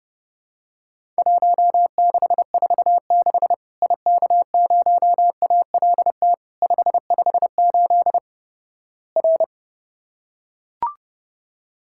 Morse code, heard as a keyed tone, about 29 wpm, 700 Hz. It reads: 1646 IK0ALT 558 R
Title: Worked All Europe DX Contest - QTC Exchange - 30wpm